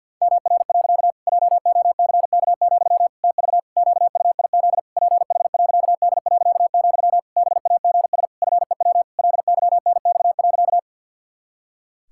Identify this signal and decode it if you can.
Morse code, heard as a keyed tone, about 50 wpm, 700 Hz. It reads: MR. JOCK, TV QUIZ PH.D., BAGS FEW LYNX.
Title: Pangram Challenge v2 - 50wpm